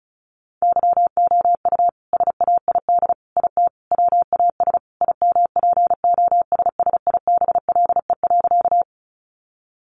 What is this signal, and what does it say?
Morse code, heard as a keyed tone, about 34 wpm, 700 Hz. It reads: YOU SAID IT WAS IMPOSSIBLE.
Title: Sentences from Top 1100 Words 35wpm